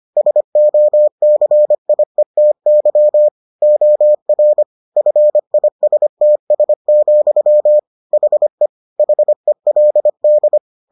Morse code, heard as a keyed tone, about 25 wpm, 600 Hz. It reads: SOCIETY OR FISTS, HE HELD